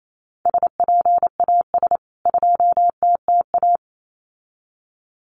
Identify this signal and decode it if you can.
Morse code, heard as a keyed tone, about 28 wpm, 700 Hz. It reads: SPAS 2TTA
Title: CWT Exchange 28wpm